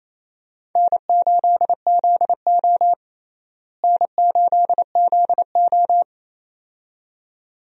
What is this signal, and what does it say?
Morse code, heard as a keyed tone, about 28 wpm, 700 Hz. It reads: N8ZO N8ZO
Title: USA Call Signs - 4 Characters - Repeated 2x - 28wpm